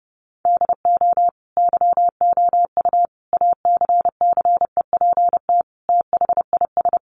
Morse code, heard as a keyed tone, about 28 wpm, 700 Hz. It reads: DO YOU ACCEPT THIS
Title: Sentences from Top 1300 Words 30wpm